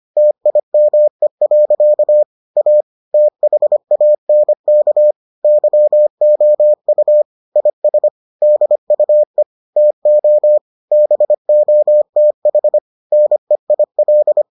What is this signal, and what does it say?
Morse code, heard as a keyed tone, about 25 wpm, 600 Hz. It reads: TIME. A THANK YOU IS DUE TO BOTH NEIL